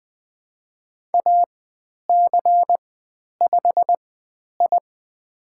Morse code, heard as a keyed tone, about 20 wpm, spaced as for 10 wpm, 700 Hz. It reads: AC5I
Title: 095 Lesson 16g - Letter c - Call Signs 20wpm